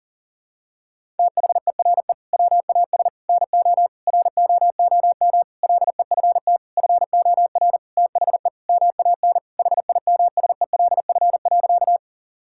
Morse code, heard as a keyed tone, about 40 wpm, 700 Hz. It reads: THERE WAS NO ROOM LEFT FOR THE MAN HIMSELF.
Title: Sentences from Top 1600 Words 40wpm